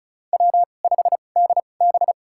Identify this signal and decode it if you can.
Morse code, heard as a keyed tone, about 36 wpm, spaced as for 25 wpm, 700 Hz. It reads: W5DB
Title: USA Call Signs - 4 Characters - 2x Character Spacing - 35wpm